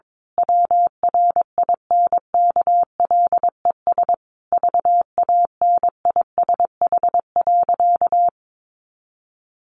Morse code, heard as a keyed tone, about 22 wpm, 700 Hz. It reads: WRINKLES VANISH.
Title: A Sound Of Thunder 22wpm